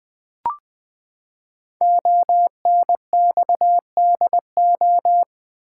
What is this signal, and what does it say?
Morse code, heard as a keyed tone, about 20 wpm, 700 Hz. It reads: ONXDO
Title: Code Groups - 5 Letters - 20wpm